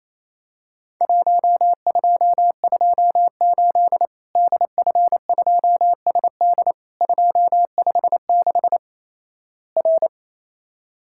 Morse code, heard as a keyed tone, about 28 wpm, 700 Hz. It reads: 1228 DF2SD 256 R
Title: Worked All Europe DX Contest - QTC Exchange - 28wpm